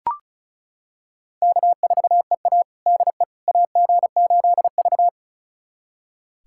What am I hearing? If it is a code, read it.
Morse code, heard as a keyed tone, about 36 wpm, 700 Hz. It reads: K4EA DE AG8V
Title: USA Callsign DE Callsign - 4 Characters - 35wpm